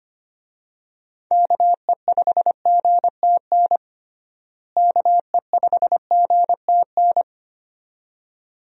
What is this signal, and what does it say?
Morse code, heard as a keyed tone, about 25 wpm, 700 Hz. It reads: KE5GTN KE5GTN
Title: USA Call Signs - 6 Characters - Repeated 2x - 25wpm